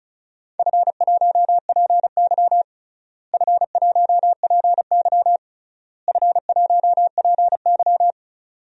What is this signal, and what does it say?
Morse code, heard as a keyed tone, about 34 wpm, 700 Hz. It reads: F1PY F1PY F1PY
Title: DXCC-100 Call Signs - 4 Characters - Repeated 3x - 35wpm